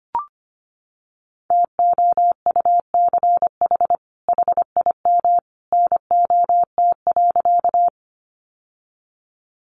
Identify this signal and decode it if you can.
Morse code, heard as a keyed tone, about 25 wpm, 700 Hz. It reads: TOUCH HIM NOT.
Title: Sentences from Top 1200 Words 25wpm